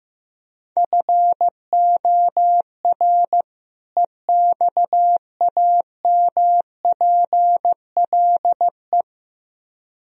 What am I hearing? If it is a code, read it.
Morse code, heard as a keyed tone, about 15 wpm, 700 Hz. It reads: FOR EXAMPLE
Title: Speed-Racing - Sets of 2 Words 15wpm